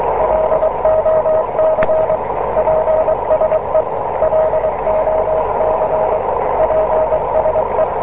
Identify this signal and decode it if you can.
Morse code, heard as a keyed tone, about 23 wpm, 630 Hz, with noise. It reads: LOC PSE LOC PSE